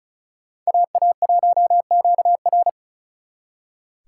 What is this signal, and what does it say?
Morse code, heard as a keyed tone, about 43 wpm, 700 Hz. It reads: AA1QR
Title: Speed Racing - USA 2x2 Call Signs 35wpm